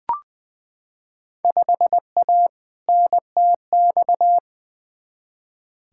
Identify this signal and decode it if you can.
Morse code, heard as a keyed tone, about 20 wpm, 700 Hz. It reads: 5A NTX
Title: Field Day Exchanges - 20wpm